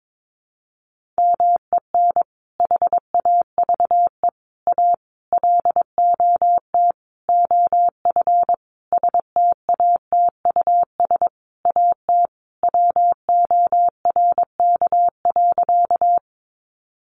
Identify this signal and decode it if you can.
Morse code, heard as a keyed tone, about 22 wpm, 700 Hz. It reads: MEN HAVE A LOT OF STATUS AT WORK.
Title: Sentences from Top 1400 Words 22wpm